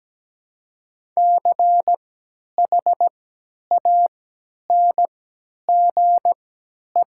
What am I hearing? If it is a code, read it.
Morse code, heard as a keyed tone, about 17 wpm, spaced as for 10 wpm, 700 Hz. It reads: CHANGE
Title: 3x Character Spacing - Top 200 Words - 17wpm